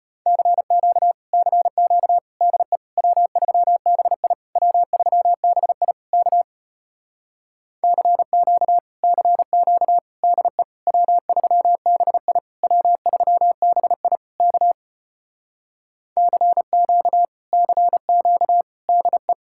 Morse code, heard as a keyed tone, about 33 wpm, 700 Hz. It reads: CQ CQ DE W3BI W3BI K CQ CQ DE W3BI W3BI K CQ CQ DE